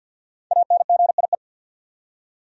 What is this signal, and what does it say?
Morse code, heard as a keyed tone, about 51 wpm, 700 Hz. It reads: ANGIE
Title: USA Common Names - Max 5 Letters - 50wpm